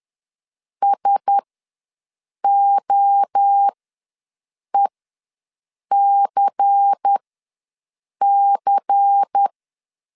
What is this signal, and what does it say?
Morse code, heard as a keyed tone, about 10 wpm, spaced as for 6 wpm, 800 Hz. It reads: SOECC
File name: LEZIONI / 5 LBVC / 5 LBVC ws 10wpm 3733